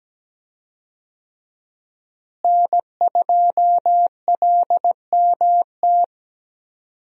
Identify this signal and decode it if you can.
Morse code, heard as a keyed tone, about 17 wpm, 700 Hz. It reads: N2LMT
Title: USA Call Signs - 5 Characters - Repeated 3x - 17wpm